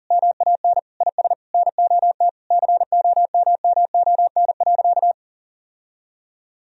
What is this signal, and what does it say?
Morse code, heard as a keyed tone, about 40 wpm, 700 Hz. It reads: MAN IS NOT COMMON.
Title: Sentences from Top 1000 Words - Review Part 3 - 40wpm